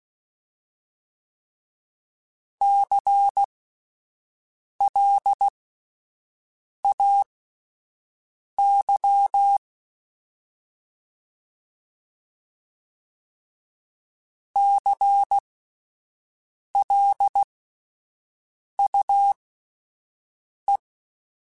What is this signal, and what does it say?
Morse code, heard as a keyed tone, about 16 wpm, spaced as for 5 wpm, 800 Hz. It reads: CLAY CLUE
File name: Spaced Alphas / Spaced Alphas mp3s / Cs sp @10